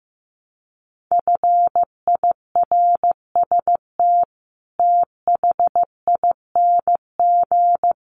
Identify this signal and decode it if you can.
Morse code, heard as a keyed tone, about 15 wpm, 700 Hz. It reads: FIRST THING
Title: Sets of 5 Words Encore2 15wpm